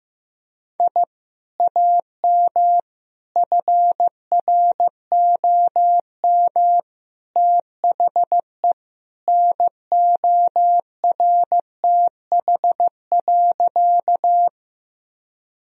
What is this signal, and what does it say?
Morse code, heard as a keyed tone, about 15 wpm, 700 Hz. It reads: I AM FROM THE NORTH.
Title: Sentences from Top 1000 Words - Review Part 1 - 15wpm